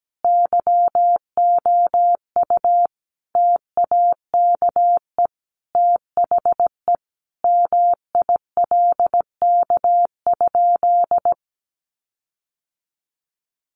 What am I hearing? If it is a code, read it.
Morse code, heard as a keyed tone, about 17 wpm, 700 Hz. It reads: YOU TAKE THE MILK?
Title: Sentences from Top 1500 Words 17wpm